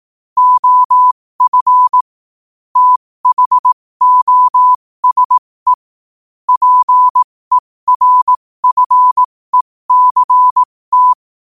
Morse code, heard as a keyed tone, about 18 wpm, spaced as for 15 wpm, 1000 Hz. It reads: OF THOSE PERFECT
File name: qotdfiles / 15wpm